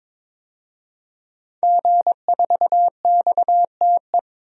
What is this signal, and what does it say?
Morse code, heard as a keyed tone, about 22 wpm, 700 Hz. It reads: G4XTE
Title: DXCC-100 Call Signs - 5 Characters - 22wpm